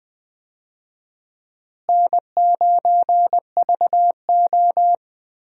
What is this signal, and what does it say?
Morse code, heard as a keyed tone, about 20 wpm, 700 Hz. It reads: N9VO
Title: USA Call Signs - 4 Characters - Repeated 2x - 20wpm